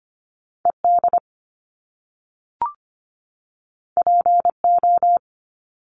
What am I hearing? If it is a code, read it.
Morse code, heard as a keyed tone, about 25 wpm, 700 Hz. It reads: ED PO
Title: Top Two Letter Combinations in Words 25wpm